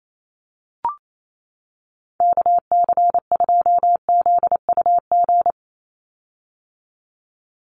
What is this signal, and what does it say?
Morse code, heard as a keyed tone, about 28 wpm, 700 Hz. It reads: KC2ZUG
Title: USA Call Signs 28wpm